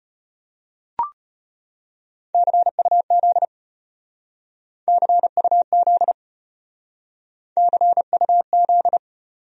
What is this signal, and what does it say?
Morse code, heard as a keyed tone, about 33 wpm, 700 Hz. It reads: CUZ CUZ CUZ